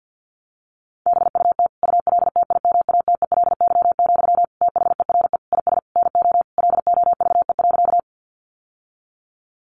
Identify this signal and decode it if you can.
Morse code, heard as a keyed tone, about 48 wpm, 700 Hz. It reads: BUT ULTIMATELY, THERE IS NO LOVE.
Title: Sentences from Top 1200 Words 50wpm